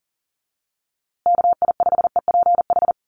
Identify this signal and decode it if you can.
Morse code, heard as a keyed tone, about 39 wpm, 700 Hz. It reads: KI5EPH
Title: USA Call Signs Repeated 2x 40wpm